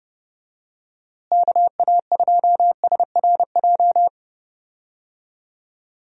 Morse code, heard as a keyed tone, about 30 wpm, 700 Hz. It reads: KA2SRJ
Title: USA Call Signs - 6 Characters - Repeated 3x - 30wpm